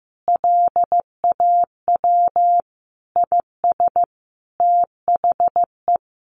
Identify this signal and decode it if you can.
Morse code, heard as a keyed tone, about 15 wpm, 700 Hz. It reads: LAW IS THE
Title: Sentences from Top 1500 Words 15wpm